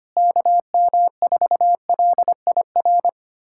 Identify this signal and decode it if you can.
Morse code, heard as a keyed tone, about 25 wpm, 700 Hz. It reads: KM4LIR
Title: USA Callsign DE Callsign 25wpm